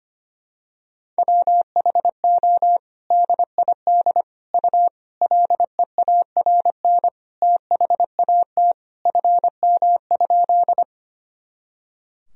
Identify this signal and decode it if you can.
Morse code, heard as a keyed tone, about 25 wpm, 700 Hz. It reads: WHO DID U LEARN THAT FM?
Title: Sentences from Top 300 Words and Abbreviations 25wpm